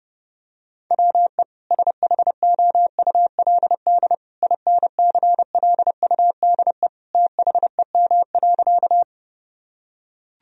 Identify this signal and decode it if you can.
Morse code, heard as a keyed tone, about 30 wpm, 700 Hz. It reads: WE SHOULD INCLUDE THEM.
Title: Sentences from Top 1000 Words - Review Part 3 - 30wpm